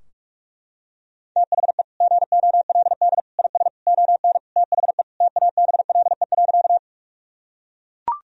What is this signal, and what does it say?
Morse code, heard as a keyed tone, about 42 wpm, 700 Hz. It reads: THE GOLD IS ON THE TABLE.
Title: Sentences from Top 1000 Words - Review Part 3 - 45wpm